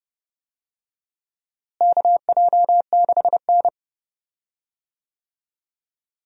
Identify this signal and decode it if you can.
Morse code, heard as a keyed tone, about 29 wpm, 700 Hz. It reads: KJ6N
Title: USA Call Signs - 4 Characters - Repeated 3x - 30wpm